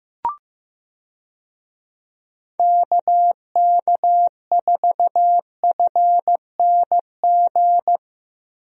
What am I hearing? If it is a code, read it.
Morse code, heard as a keyed tone, about 15 wpm, 700 Hz. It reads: KK4FNG
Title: USA Call Signs - 6 Characters - 15wpm